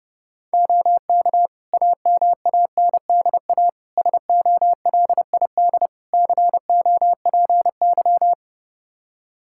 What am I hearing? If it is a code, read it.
Morse code, heard as a keyed tone, about 30 wpm, 700 Hz. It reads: OK AMANDA SOLID COPY
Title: K1USN SST QSO Practice 30wpm